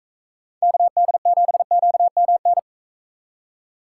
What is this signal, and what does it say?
Morse code, heard as a keyed tone, about 42 wpm, 700 Hz. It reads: KD7QMN